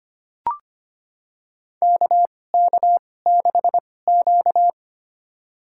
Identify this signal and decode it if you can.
Morse code, heard as a keyed tone, about 25 wpm, spaced as for 18 wpm, 700 Hz. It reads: KK6Q
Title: USA Call Signs - 4 Characters - 2x Character Spacing - 25wpm